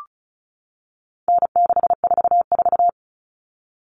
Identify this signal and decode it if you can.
Morse code, heard as a keyed tone, about 34 wpm, 700 Hz. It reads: N644
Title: All Cut Numbers - 4-Digit 35wpm